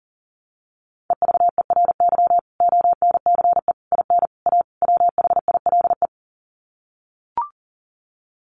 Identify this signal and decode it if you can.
Morse code, heard as a keyed tone, about 40 wpm, 700 Hz. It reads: EVERY ONCE IN A WHILE
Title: Sets of 5 Words 40wpm